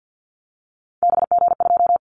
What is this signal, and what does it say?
Morse code, heard as a keyed tone, about 48 wpm, 700 Hz. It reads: BG2
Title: Callsign Prefixes for Top 100 DXCC - 50wpm